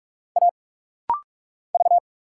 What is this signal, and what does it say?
Morse code, heard as a keyed tone, about 46 wpm, 690 Hz. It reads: A V
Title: Single Letters - Rapid-Fire 45wpm